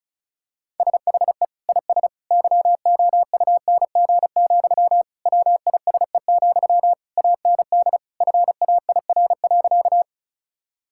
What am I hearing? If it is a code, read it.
Morse code, heard as a keyed tone, about 36 wpm, 700 Hz. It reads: SHE IS YOUNG, WISE, AND FAIR.
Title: Sentences from Top 1600 Words 35wpm